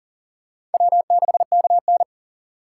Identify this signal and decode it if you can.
Morse code, heard as a keyed tone, about 41 wpm, 700 Hz. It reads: W6KN
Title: USA Call Signs - 4 Characters - 3x Character Spacing - 40wpm